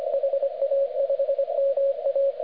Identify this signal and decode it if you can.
Morse code, heard as a keyed tone, about 24 wpm, 550 Hz, with radio noise. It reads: HA5MA